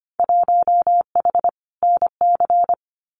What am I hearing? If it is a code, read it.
Morse code, heard as a keyed tone, about 25 wpm, 700 Hz. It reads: 1H NC
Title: Winter Field Day Exchanges 25wpm